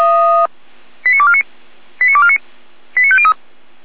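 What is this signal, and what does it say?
Five-tone selective call (ZVEI), heard as quick runs of five tones, tones of about 70 ms. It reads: 89249 89249 89503
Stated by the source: ZVEI five-tone selective call received on an SDR.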